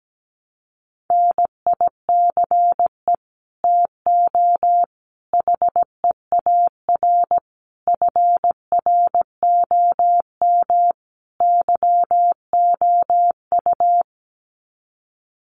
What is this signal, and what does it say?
Morse code, heard as a keyed tone, about 17 wpm, 700 Hz. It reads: NICE TO HEAR FROM YOU
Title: Common QSO Phrases 17wpm